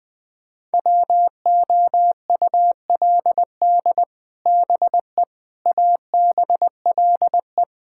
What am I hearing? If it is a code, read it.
Morse code, heard as a keyed tone, about 20 wpm, 700 Hz. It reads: WOULD BE ABLE
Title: Build A Phrase - 4-Word Phrases - 20wpm